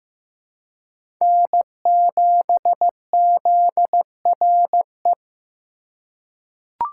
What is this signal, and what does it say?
Morse code, heard as a keyed tone, about 15 wpm, 700 Hz. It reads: N7ZRE
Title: USA Call Signs - 5 Characters - 3x Character Spacing - 15wpm